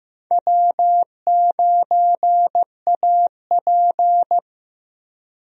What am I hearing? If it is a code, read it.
Morse code, heard as a keyed tone, about 15 wpm, 700 Hz. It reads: W9AP
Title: USA Call Signs - 4 Characters - Repeated 3x - 15wpm